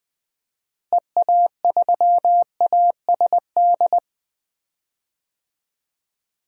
Morse code, heard as a keyed tone, about 20 wpm, 700 Hz. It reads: EA3ASD
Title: DXCC-100 Call Signs - 6 Characters - Repeated 3x - 20wpm